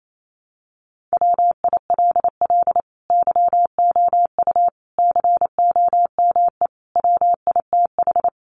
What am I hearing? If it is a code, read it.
Morse code, heard as a keyed tone, about 28 wpm, 700 Hz. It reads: WILL YOU COME WITH
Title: Sentences from Top 200 Words 28wpm